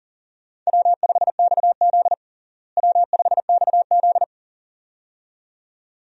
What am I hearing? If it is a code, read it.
Morse code, heard as a keyed tone, about 40 wpm, 700 Hz. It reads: W5XZ W5XZ
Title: USA Call Signs - 4 Characters - Repeated 2x - 40wpm